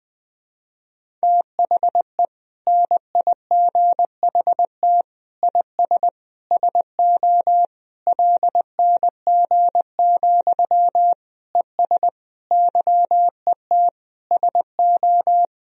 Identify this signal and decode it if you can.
Morse code, heard as a keyed tone, about 20 wpm, 700 Hz. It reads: THE NIGHT IS SO LNG, ES YET SO
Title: Sentences from Top 500 Words and Abbreviations 20wpm